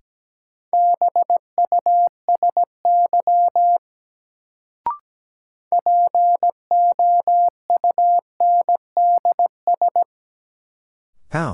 Morse code, heard as a keyed tone, about 17 wpm, 700 Hz. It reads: BUSY POUNDS
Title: Sentences from Top 1000 Words - Review Part 2 - New Word Drill - 17wpm